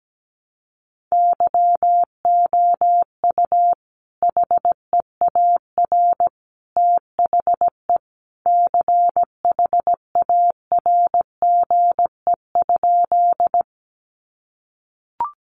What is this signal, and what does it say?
Morse code, heard as a keyed tone, about 17 wpm, 700 Hz. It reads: YOU HEAR THE CHARGE?
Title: Sentences from Top 900 Words 17wpm